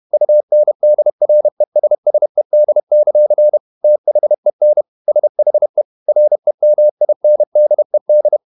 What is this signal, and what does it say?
Morse code, heard as a keyed tone, about 31 wpm, 600 Hz. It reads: UNDRESSED; THEN SHE REMINDED
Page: Never 31wpm | CW med Gnister